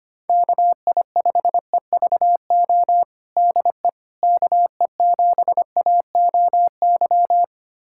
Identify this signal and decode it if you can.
Morse code, heard as a keyed tone, about 25 wpm, 700 Hz. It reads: KI5EVO DE KE7AOY
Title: USA Callsign DE Callsign 25wpm